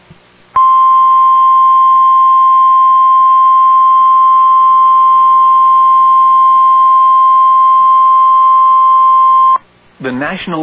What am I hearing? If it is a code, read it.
Emergency Alert System (SAME) tones (the 1050 Hz attention tone), with speech. It is heard from 0.5 to 9.6 s.